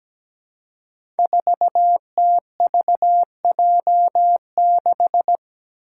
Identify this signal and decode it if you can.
Morse code, heard as a keyed tone, about 17 wpm, 700 Hz. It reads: 4TVJ6
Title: Five Challenging Letter-Number Combinations 17wpm